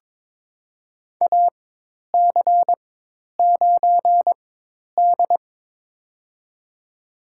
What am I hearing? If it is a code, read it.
Morse code, heard as a keyed tone, about 22 wpm, spaced as for 10 wpm, 700 Hz. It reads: AC9D
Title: USA Call Signs - 4 Characters - 4x Character Spacing - 22wpm